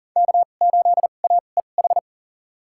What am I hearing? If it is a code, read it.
Morse code, heard as a keyed tone, about 40 wpm, spaced as for 29 wpm, 700 Hz. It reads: K8AEH
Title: USA Call Signs - 5 Characters - 2x Character Spacing - 40wpm